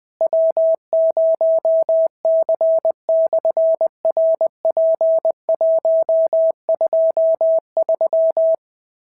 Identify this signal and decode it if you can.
Morse code, heard as a keyed tone, about 20 wpm, 650 Hz. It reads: W0C/RP123
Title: SOTA QSO Practice 20wpm